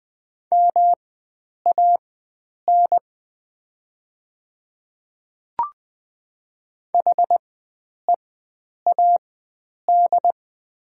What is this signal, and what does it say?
Morse code, heard as a keyed tone, about 20 wpm, spaced as for 9 wpm, 700 Hz. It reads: MAN HEAD
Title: 4x Character Spacing - Top 300 Words - 20wpm